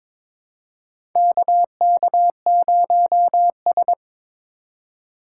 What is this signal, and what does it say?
Morse code, heard as a keyed tone, about 22 wpm, 700 Hz. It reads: KK0S
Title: USA Call Signs - 4 Characters - 2x Character Spacing - 22wpm